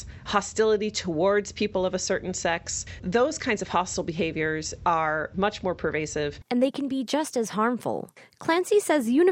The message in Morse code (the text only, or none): none